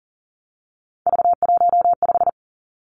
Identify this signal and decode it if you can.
Morse code, heard as a keyed tone, about 39 wpm, 700 Hz. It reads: V15
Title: Three Challenging Letter-Number Combinations 40wpm